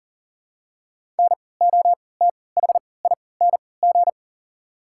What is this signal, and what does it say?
Morse code, heard as a keyed tone, about 40 wpm, spaced as for 23 wpm, 700 Hz. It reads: NOTHING
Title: 3x Character Spacing - Top 500 Words - 40wpm